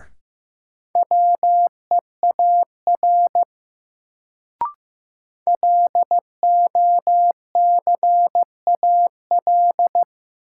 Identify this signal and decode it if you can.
Morse code, heard as a keyed tone, about 15 wpm, 700 Hz. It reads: WEAR LOCAL
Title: Sentences from Top 1500 Words - New Word Drill Alt - 15wpm